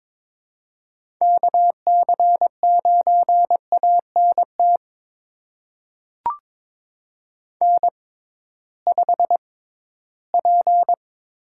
Morse code, heard as a keyed tone, about 22 wpm, spaced as for 14 wpm, 700 Hz. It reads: KC9ANT N5P
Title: USA Call Signs - 6x Character Spacing - 22wpm